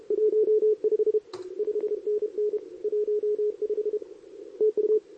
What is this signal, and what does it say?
Morse code, heard as a keyed tone, about 33 wpm, 440 Hz, with radio noise. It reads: 15 5NN 15 TU
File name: CQWWCW2023 / SP